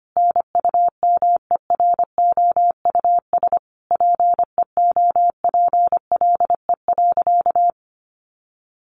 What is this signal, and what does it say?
Morse code, heard as a keyed tone, about 25 wpm, 700 Hz. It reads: NUMEROUS PEOPLE.
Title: Sentences from Top 1400 Words 25wpm